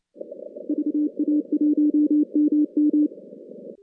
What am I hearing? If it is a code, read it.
Morse code, heard as a keyed tone, about 29 wpm, 300 Hz. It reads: VA1MM
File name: contests / CW-SS-k5tr- -2100